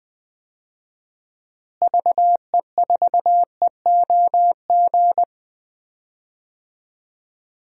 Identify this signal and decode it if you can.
Morse code, heard as a keyed tone, about 20 wpm, 700 Hz. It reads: VE4EOG
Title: DXCC-100 Call Signs - 6 Characters - Repeated 3x - 20wpm